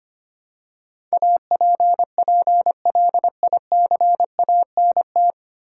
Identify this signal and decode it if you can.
Morse code, heard as a keyed tone, about 25 wpm, 700 Hz. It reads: APPLICANT
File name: Nine Letter Words - 25wpm